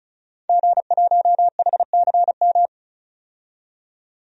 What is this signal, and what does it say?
Morse code, heard as a keyed tone, about 36 wpm, 700 Hz. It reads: G1HCM
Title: DXCC-100 Call Signs - 5 Characters - 35wpm